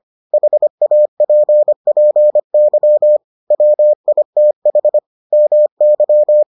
Morse code, heard as a keyed tone, about 25 wpm, 600 Hz. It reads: HAPPY WITH MY